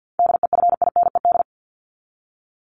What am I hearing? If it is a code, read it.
Morse code, heard as a keyed tone, about 50 wpm, 700 Hz. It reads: DEFINED
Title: ED Suffix - Top 100 Words with Base Words 50wpm